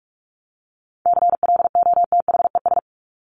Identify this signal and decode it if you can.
Morse code, heard as a keyed tone, about 44 wpm, 700 Hz. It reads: CLOTHES
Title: Top 1751-2000 Words 45wpm